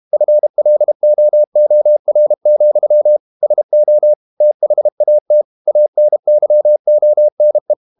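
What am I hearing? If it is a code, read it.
Morse code, heard as a keyed tone, about 32 wpm, 600 Hz. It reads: FLOOR, SO THAT ANYONE
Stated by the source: Morse code practice at 32 wpm from CW med Gnister, reading Ken Follett's novel Never in English.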